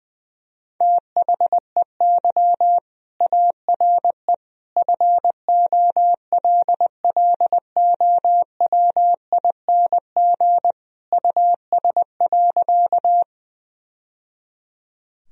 Morse code, heard as a keyed tone, about 20 wpm, 700 Hz. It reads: THEY ARE FOLLOWING US.
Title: Sentences from Top 1000 Words - Review Part 4 - 20wpm